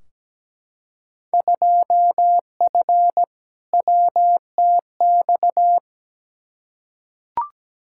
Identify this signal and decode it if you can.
Morse code, heard as a keyed tone, about 17 wpm, 700 Hz. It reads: 2F WTX
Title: Field Day Exchanges - 17wpm